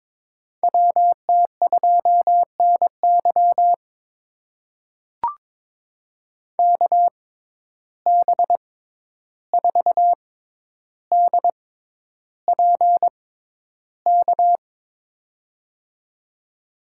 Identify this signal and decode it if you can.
Morse code, heard as a keyed tone, about 21 wpm, spaced as for 11 wpm, 700 Hz. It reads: WT2NY KB4DPK
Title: USA Call Signs - 6x Character Spacing - 22wpm